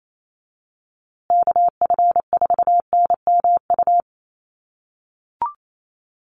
Morse code, heard as a keyed tone, about 28 wpm, 700 Hz. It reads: KF4NMU
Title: USA Call Signs Repeated 3x 28wpm